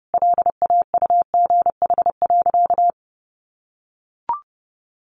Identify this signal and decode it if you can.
Morse code, heard as a keyed tone, about 30 wpm, 700 Hz. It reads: LAUGH.
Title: Sentences from Top 1100 Words 30wpm